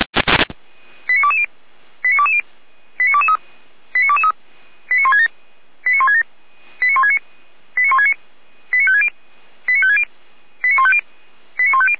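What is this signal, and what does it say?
Five-tone selective call (ZVEI), heard as quick runs of five tones, tones of about 70 ms. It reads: 89220 89220 89223 89223 89167 89167 89169 89169 89560 89560 89150 89150
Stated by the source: ZVEI five-tone call received by SDR.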